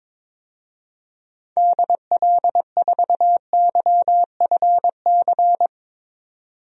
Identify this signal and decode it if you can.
Morse code, heard as a keyed tone, about 22 wpm, 700 Hz. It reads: DL4YFC
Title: DXCC-100 Call Signs - 6 Characters - 22wpm